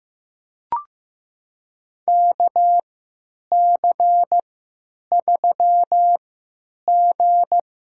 Morse code, heard as a keyed tone, about 16 wpm, spaced as for 9 wpm, 700 Hz. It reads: KC3G
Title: USA Call Signs - 6 Characters - 3x Character Spacing - 15wpm